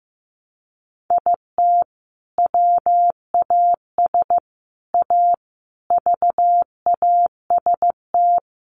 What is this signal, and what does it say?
Morse code, heard as a keyed tone, about 15 wpm, 700 Hz. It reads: IT WAS A VAST
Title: Sentences from Top 1300 Words 15wpm